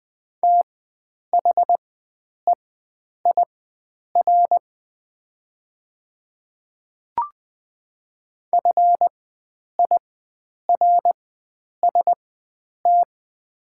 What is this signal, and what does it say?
Morse code, heard as a keyed tone, about 20 wpm, spaced as for 9 wpm, 700 Hz. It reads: THEIR FIRST
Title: 4x Character Spacing - Top 100 Words - 20wpm